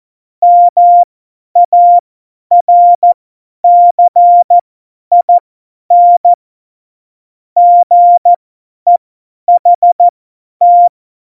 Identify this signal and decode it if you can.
Morse code, heard as a keyed tone, about 14 wpm, spaced as for 10 wpm, 700 Hz. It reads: MARCIN GEHT
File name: Morsekurs / Witze / Witz 12. Dezember 2017, 50 F70